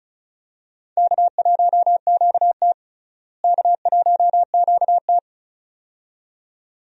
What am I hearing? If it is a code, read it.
Morse code, heard as a keyed tone, about 34 wpm, 700 Hz. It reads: K1QT K1QT
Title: USA Call Signs - 4 Characters - Repeated 2x - 35wpm